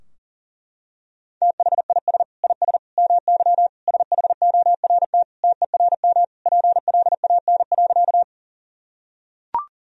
Morse code, heard as a keyed tone, about 40 wpm, 700 Hz. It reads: THIS IS MY SHORT TERM PLAN.
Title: Sentences from Top 1000 Words - Review Part 4 - 40wpm